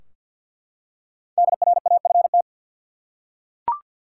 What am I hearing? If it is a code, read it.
Morse code, heard as a keyed tone, about 49 wpm, 700 Hz. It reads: DRAFT
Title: Five Letter Words - 50wpm